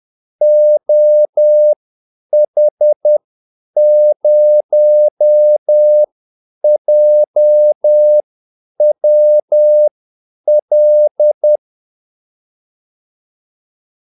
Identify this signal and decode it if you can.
Morse code, heard as a keyed tone, about 10 wpm, spaced as for 8 wpm, 600 Hz. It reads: OH0JWL
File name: Callsigns-10-8wpm